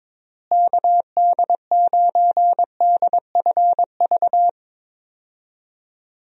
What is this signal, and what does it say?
Morse code, heard as a keyed tone, about 22 wpm, 700 Hz. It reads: KD9DFV
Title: USA Call Signs - 6 Characters - Repeated 3x - 22wpm